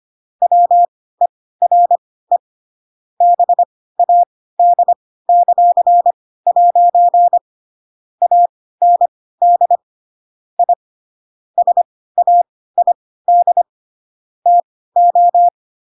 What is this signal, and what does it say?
Morse code, heard as a keyed tone, about 25 wpm, spaced as for 16 wpm, 700 Hz. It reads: WERE BAD;' AND I SAID TO
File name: Pinocchio / Pinocchio ch7 10 25 ik1hge